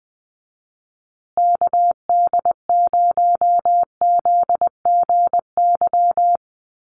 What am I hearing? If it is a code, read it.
Morse code, heard as a keyed tone, about 20 wpm, 700 Hz. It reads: KD0ZGY
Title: USA Call Signs Repeated 3x 20wpm